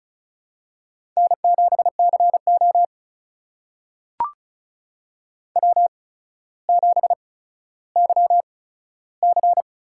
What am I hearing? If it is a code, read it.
Morse code, heard as a keyed tone, about 36 wpm, spaced as for 15 wpm, 700 Hz. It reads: N7CO W7YC
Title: USA Call Signs - 4 Characters - 8x Character Spacing - 35wpm